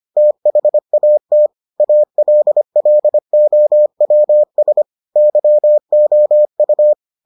Morse code, heard as a keyed tone, about 25 wpm, 600 Hz. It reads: THAT ALLOWS YOU